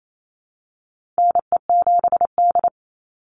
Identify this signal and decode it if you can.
Morse code, heard as a keyed tone, about 28 wpm, 700 Hz. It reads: NE7D
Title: Call Signs - 28wpm